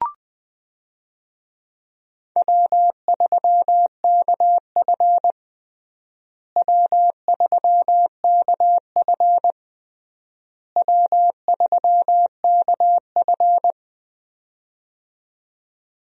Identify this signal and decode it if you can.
Morse code, heard as a keyed tone, about 19 wpm, 700 Hz. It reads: W3KF W3KF W3KF
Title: USA Call Signs - 4 Characters - Repeated 3x - 20wpm